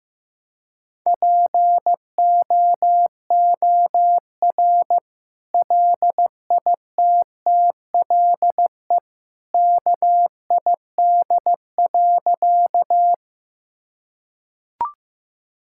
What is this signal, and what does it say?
Morse code, heard as a keyed tone, about 15 wpm, 700 Hz. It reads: POOR LITTLE KID.
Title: Sentences from Top 1600 Words 15wpm